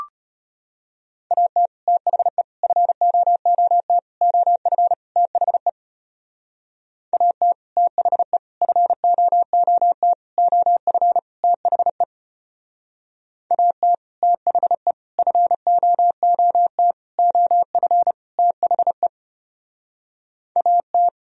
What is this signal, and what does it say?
Morse code, heard as a keyed tone, about 32 wpm, 700 Hz. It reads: AT THE FOOT OF THE AT THE FOOT OF THE AT THE FOOT OF THE AT